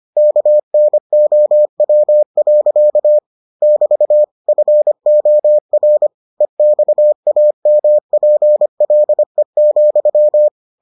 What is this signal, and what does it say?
Morse code, heard as a keyed tone, about 25 wpm, 600 Hz. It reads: KNOW. <BT> FOR EXAMPLE,